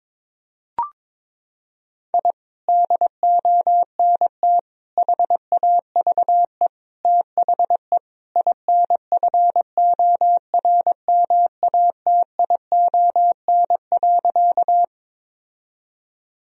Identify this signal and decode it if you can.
Morse code, heard as a keyed tone, about 22 wpm, 700 Hz. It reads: I DONT HAVE THE INFORMATION.
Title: Sentences from Top 1000 Words - Review Part 3 - 22wpm